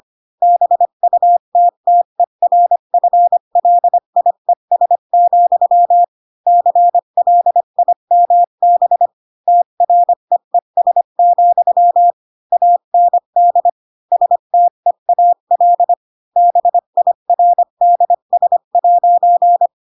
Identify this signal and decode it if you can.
Morse code, heard as a keyed tone, about 25 wpm, 700 Hz. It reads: BUTTERFLIES, CLIMB TREES, AND STEAL BIRDS'
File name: Pinocchio / Pinocchio ch4 20 25 ik1hge